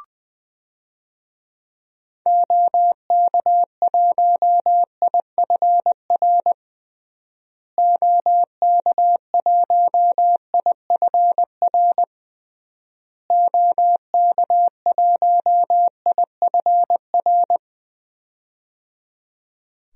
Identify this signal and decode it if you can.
Morse code, heard as a keyed tone, about 20 wpm, 700 Hz. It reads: OK1IFR OK1IFR OK1IFR
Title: DXCC-100 Call Signs - 6 Characters - Repeated 3x - 20wpm